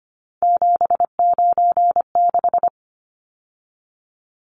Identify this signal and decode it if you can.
Morse code, heard as a keyed tone, about 25 wpm, 700 Hz. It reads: 796
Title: Numbers - 3-Digit 25wpm